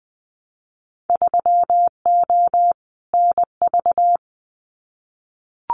Morse code, heard as a keyed tone, about 20 wpm, 700 Hz. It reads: 3O NV
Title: Winter Field Day Exchanges Encore 20wpm